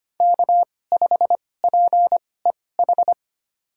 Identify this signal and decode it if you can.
Morse code, heard as a keyed tone, about 25 wpm, spaced as for 18 wpm, 700 Hz. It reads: K5PEH
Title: USA Call Signs - 2x Character Spacing - 25wpm